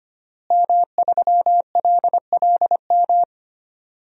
Morse code, heard as a keyed tone, about 25 wpm, 700 Hz. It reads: M3LLM
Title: DXCC-100 Call Signs - 5 Characters - Repeated 2x - 25wpm